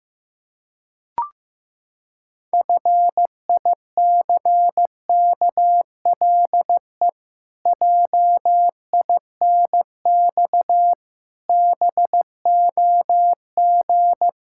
Morse code, heard as a keyed tone, about 16 wpm, 700 Hz. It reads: FICKLE JINX BOG
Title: Pangram Challenge v2 - 15wpm